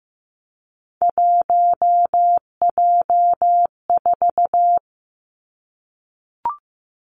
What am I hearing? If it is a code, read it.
Morse code, heard as a keyed tone, about 15 wpm, 700 Hz. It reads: 1J4
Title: Three Challenging Letter-Number Combinations 15wpm